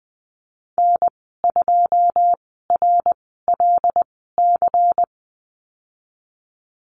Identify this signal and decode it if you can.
Morse code, heard as a keyed tone, about 20 wpm, spaced as for 15 wpm, 700 Hz. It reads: N2RLC
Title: 186 Lesson 30g - Letter b - Call Signs 20wpm